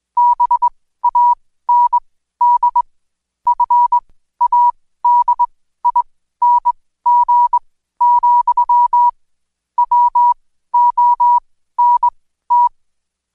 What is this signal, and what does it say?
Morse code, heard as a keyed tone, about 21 wpm, spaced as for 14 wpm, 970 Hz. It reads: BAND FADING, WONT
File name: Code Quick / AudioDownld2 / Seg124-Day29-DayDrill 29